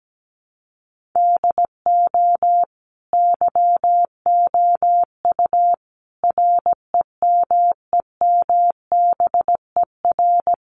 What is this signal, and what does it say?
Morse code, heard as a keyed tone, about 17 wpm, 700 Hz. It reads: DO YOU REMEMBER
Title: Sentences from Top 400 Words 17wpm